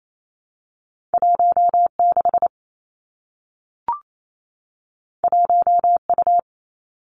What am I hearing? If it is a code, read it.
Morse code, heard as a keyed tone, about 28 wpm, 700 Hz. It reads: 1 6 1 U
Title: Two Challenging Letter-Number Combinations 28wpm